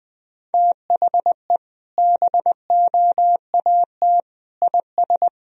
Morse code, heard as a keyed tone, about 20 wpm, 700 Hz. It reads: THE BOAT IS
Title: Sentences from Top 1000 Words - Review Part 1 - 20wpm